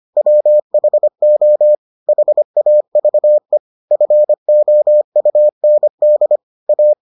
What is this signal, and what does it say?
Morse code, heard as a keyed tone, about 25 wpm, 600 Hz. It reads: WHO HAVE FOUND A